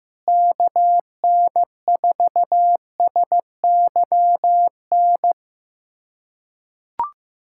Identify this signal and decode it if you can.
Morse code, heard as a keyed tone, about 15 wpm, 700 Hz. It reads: KN4SYN
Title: USA Callsign DE Callsign - 6 Characters - 15wpm